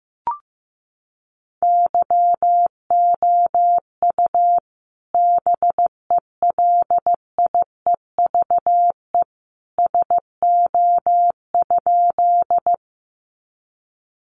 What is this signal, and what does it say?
Morse code, heard as a keyed tone, about 15 wpm, 700 Hz. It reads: YOU BELIEVE SO?
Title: Sentences from Top 1000 Words 15wpm